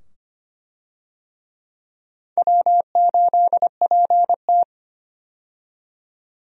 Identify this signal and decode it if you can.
Morse code, heard as a keyed tone, about 25 wpm, 700 Hz. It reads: W8PT
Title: USA Call Signs - 4 Characters - 25wpm